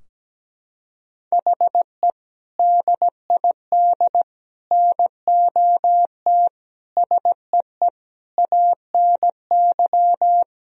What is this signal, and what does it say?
Morse code, heard as a keyed tone, about 17 wpm, 700 Hz. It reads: HE DID NOT SEE ANY
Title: Sentences from Top 1000 Words - Review Part 3 - 17wpm